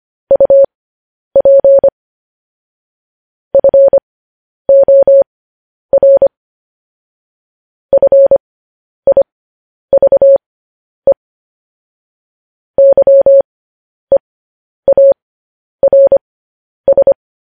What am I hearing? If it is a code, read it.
Morse code, heard as a keyed tone, about 25 wpm, spaced as for 10 wpm, 550 Hz. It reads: UP FOR FIVE YEARS
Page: news-headlines-in-morse-code-at-10-wpm